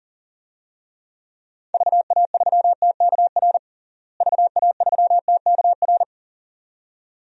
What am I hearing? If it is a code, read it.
Morse code, heard as a keyed tone, about 40 wpm, 700 Hz. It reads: VA3TKR VA3TKR
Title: DXCC-100 Call Signs - 6 Characters - Repeated 2x - 40wpm